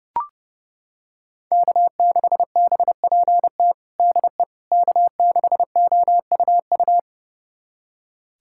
Morse code, heard as a keyed tone, about 30 wpm, 700 Hz. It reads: K6BPT DE K6OUU
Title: USA Callsign DE Callsign - 5 Characters - 30wpm